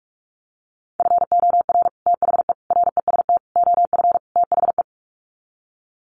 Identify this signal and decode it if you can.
Morse code, heard as a keyed tone, about 44 wpm, 700 Hz. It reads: FOR THE REST OF THE
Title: Sets of 5 Words 45wpm